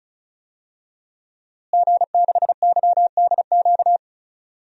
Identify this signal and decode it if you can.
Morse code, heard as a keyed tone, about 36 wpm, 700 Hz. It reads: G6YDQ
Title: DXCC-100 Call Signs - 5 Characters - Repeated 2x - 35wpm